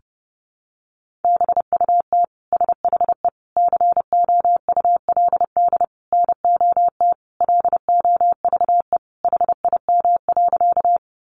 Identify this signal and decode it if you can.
Morse code, heard as a keyed tone, about 30 wpm, 700 Hz. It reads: BUT SHE COULD NOT LOVE HIM.
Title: Sentences from Top 600 Words 30wpm